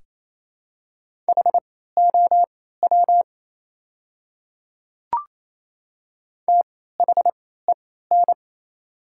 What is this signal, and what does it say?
Morse code, heard as a keyed tone, about 27 wpm, spaced as for 16 wpm, 690 Hz. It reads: HOW THEN
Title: 3x Character Spacing - Top 100 Words - 28wpm